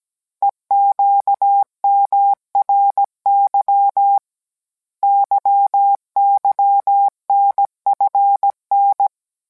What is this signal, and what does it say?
Morse code, heard as a keyed tone, about 17 wpm, 800 Hz. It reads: EQMRY YYNFN